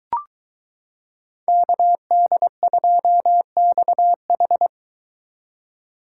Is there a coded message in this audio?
Yes: Morse code from 1.5 to 4.7 s, heard as a keyed tone, about 22 wpm, 700 Hz. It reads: KD2XH